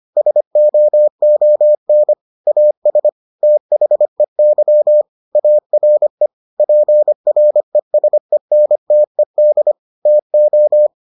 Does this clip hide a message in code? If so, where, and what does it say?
Yes: Morse code from 0.2 to 10.9 s, heard as a keyed tone, about 25 wpm, 600 Hz. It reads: SOON AS THEY ARE PRESENTED TO